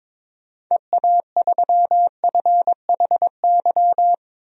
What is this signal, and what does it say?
Morse code, heard as a keyed tone, about 22 wpm, 700 Hz. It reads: EA3FHY
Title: DXCC-100 Call Signs - 6 Characters - 22wpm